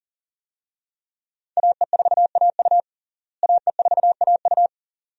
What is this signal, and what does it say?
Morse code, heard as a keyed tone, about 39 wpm, 700 Hz. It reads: AE4AU AE4AU
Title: USA Call Signs - 5 Characters - Repeated 2x - 40wpm